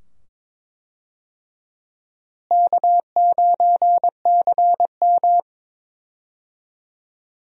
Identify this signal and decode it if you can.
Morse code, heard as a keyed tone, about 21 wpm, 700 Hz. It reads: K9CM
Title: USA Call Signs - 4 Characters - Repeated 2x - 22wpm